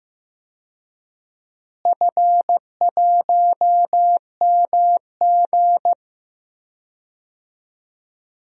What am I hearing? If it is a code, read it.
Morse code, heard as a keyed tone, about 15 wpm, 700 Hz. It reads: F1MG
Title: DXCC-100 Call Signs - 4 Characters - Repeated 3x - 15wpm